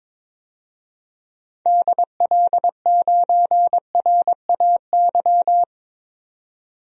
Morse code, heard as a keyed tone, about 22 wpm, 700 Hz. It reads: DL9RAY
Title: DXCC-100 Call Signs - 6 Characters - 22wpm